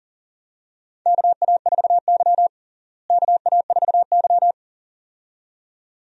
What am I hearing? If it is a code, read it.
Morse code, heard as a keyed tone, about 40 wpm, 700 Hz. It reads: KA4Y KA4Y
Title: USA Call Signs - 4 Characters - Repeated 2x - 40wpm